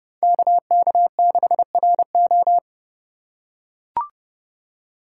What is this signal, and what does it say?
Morse code, heard as a keyed tone, about 30 wpm, 700 Hz. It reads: KK6RO
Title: USA Call Signs - 5 Characters - 4x Character Spacing - 30wpm